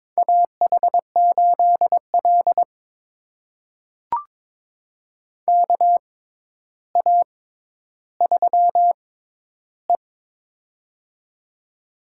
Morse code, heard as a keyed tone, about 22 wpm, spaced as for 11 wpm, 700 Hz. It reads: AH8L KA3E
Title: USA Call Signs - 4 Characters - 6x Character Spacing - 22wpm